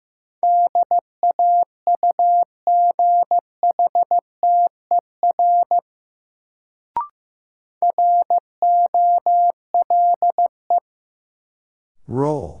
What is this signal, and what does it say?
Morse code, heard as a keyed tone, about 15 wpm, 700 Hz. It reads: DAUGHTER ROLE
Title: Sentences from Top 1500 Words - New Word Drill Alt - 15wpm